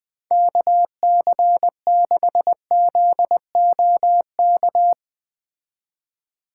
Morse code, heard as a keyed tone, about 20 wpm, 700 Hz. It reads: KC6ZOK
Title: Speed Racing - USA 2x3 Call Signs 17wpm